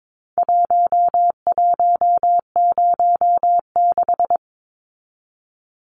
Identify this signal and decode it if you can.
Morse code, heard as a keyed tone, about 22 wpm, 700 Hz. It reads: 1106
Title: Numbers - Contest Style - 4-Digit 22wpm